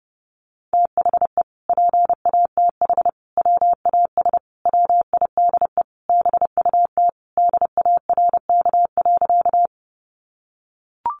Morse code, heard as a keyed tone, about 30 wpm, 700 Hz. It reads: THE PATH WAS WIDE BUT DARK.
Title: Sentences from Top 1400 Words 30wpm